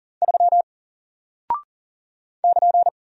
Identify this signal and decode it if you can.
Morse code, heard as a keyed tone, about 40 wpm, 700 Hz. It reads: 3 <KN>
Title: Single Character (Letter:Number:Punctuation:Prosign) 40wpm